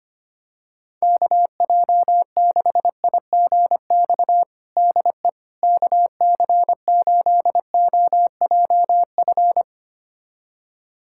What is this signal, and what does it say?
Morse code, heard as a keyed tone, about 25 wpm, 700 Hz. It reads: KJ6IGX DE KC8OJF
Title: USA Callsign DE Callsign - 6 Characters - 25wpm